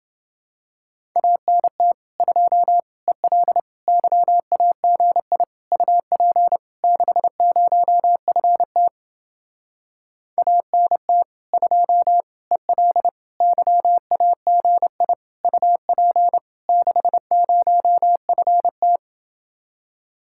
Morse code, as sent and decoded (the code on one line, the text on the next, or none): .- -. - / ..--- / . .-.. / -.-- .- --. .. / ..- .--. / -.... ----- ..-. - / .- -. - / ..--- / . .-.. / -.-- .- --. .. / ..- .--. / -.... ----- ..-. -
ANT 2 EL YAGI UP 60FT ANT 2 EL YAGI UP 60FT